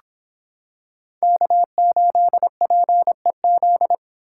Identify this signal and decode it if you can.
Morse code, heard as a keyed tone, about 26 wpm, 700 Hz. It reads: K8PEZ